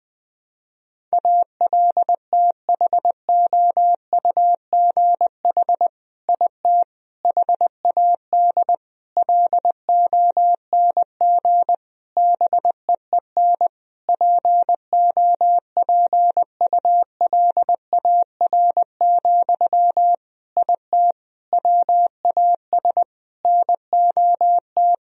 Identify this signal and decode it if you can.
Morse code, heard as a keyed tone, about 20 wpm, 700 Hz. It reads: ALTHOUGH IT HAD LONG BEEN POPULAR, IT WAS NOT
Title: Sentences from Top 1000 Words - Review Part 3 - 20wpm